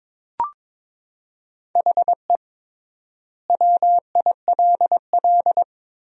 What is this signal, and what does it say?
Morse code, heard as a keyed tone, about 22 wpm, 700 Hz. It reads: HE WILL
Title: 3x Word-Spacing - Sentences from Top 500 Words - 22wpm